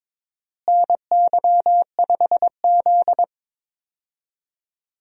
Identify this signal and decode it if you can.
Morse code, heard as a keyed tone, about 22 wpm, 700 Hz. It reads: NY5Z
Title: USA Call Signs - 4 Characters - Repeated 3x - 22wpm